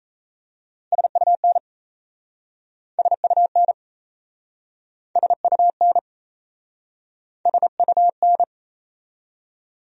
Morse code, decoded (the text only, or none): SUN SUN SUN SUN